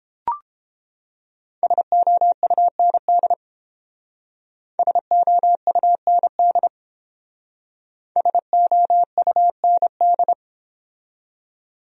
Morse code, heard as a keyed tone, about 29 wpm, 700 Hz. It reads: SOUND SOUND SOUND